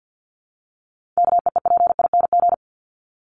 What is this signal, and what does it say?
Morse code, heard as a keyed tone, about 50 wpm, 700 Hz. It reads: KEEPING
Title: ING Suffix - Top 100 Words with Base Words 50wpm